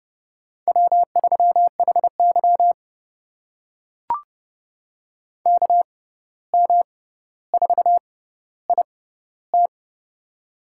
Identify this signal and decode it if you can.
Morse code, heard as a keyed tone, about 30 wpm, spaced as for 14 wpm, 700 Hz. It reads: W3HY KM4IT
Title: USA Call Signs - 6x Character Spacing - 30wpm